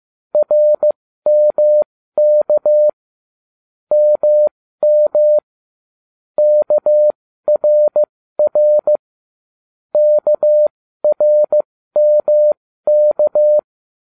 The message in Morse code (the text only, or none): RMK MM KRR KRMK